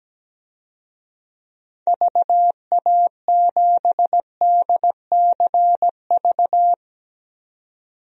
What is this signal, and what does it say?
Morse code, heard as a keyed tone, about 17 wpm, 700 Hz. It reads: VA7DCV
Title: DXCC-100 Call Signs - 6 Characters - 17wpm